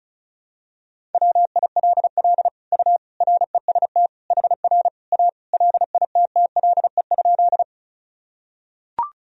Morse code, heard as a keyed tone, about 34 wpm, 700 Hz. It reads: WILL U REST HR A LITTLE?
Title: Sentences from Top 500 Words and Abbreviations 35wpm